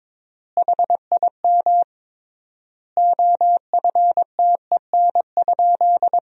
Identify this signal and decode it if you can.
Morse code, heard as a keyed tone, about 22 wpm, 700 Hz. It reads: HIM OFTEN?
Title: 3x Word-Spacing - Sentences from Top 400 Words - 22wpm